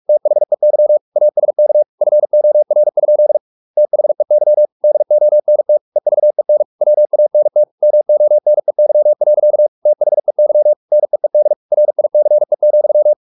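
Morse code, heard as a keyed tone, about 44 wpm, 600 Hz. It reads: THEY ASK FOR? THEY DONT EVEN WANT MONEY. THEY NEED RICE,
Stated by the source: Morse practice at 45 wpm, the novel Never by Ken Follett in English, from CW med Gnister.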